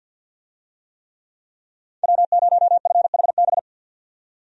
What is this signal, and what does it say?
Morse code, heard as a keyed tone, about 47 wpm, 700 Hz. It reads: W0FHB
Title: USA Call Signs - 5 Characters - 50wpm